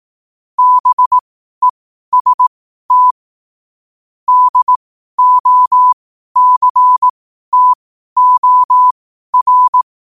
Morse code, heard as a keyed tone, about 18 wpm, spaced as for 12 wpm, 1000 Hz. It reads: BEST DOCTOR
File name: qotdfiles / 12wpm